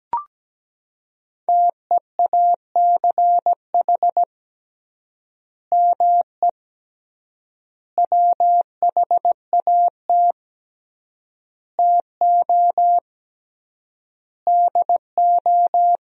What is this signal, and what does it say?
Morse code, heard as a keyed tone, about 17 wpm, 700 Hz. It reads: TEACH ME WHAT TO DO
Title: 3x Word-Spacing - Sentences from Top 500 Words - 17wpm